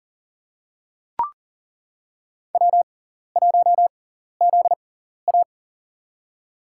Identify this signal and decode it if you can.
Morse code, heard as a keyed tone, about 40 wpm, spaced as for 14 wpm, 700 Hz. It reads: W1ZA
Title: USA Call Signs - 4 Characters - 6x Character Spacing - 40wpm